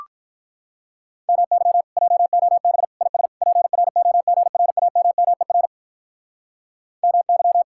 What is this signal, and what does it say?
Morse code, heard as a keyed tone, about 46 wpm, 700 Hz. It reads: MY JOB IS PROGRAMMER MY